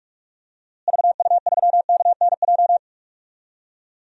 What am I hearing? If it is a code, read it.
Morse code, heard as a keyed tone, about 46 wpm, 700 Hz. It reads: VU3KNJ
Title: DXCC-100 Call Signs - 6 Characters - Repeated 2x - 45wpm